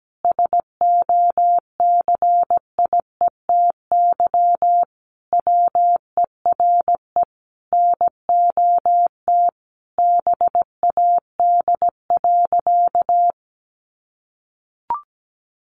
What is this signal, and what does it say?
Morse code, heard as a keyed tone, about 17 wpm, 700 Hz. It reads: SOCIETY WERE NOT BAD.
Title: Sentences from Top 900 Words 17wpm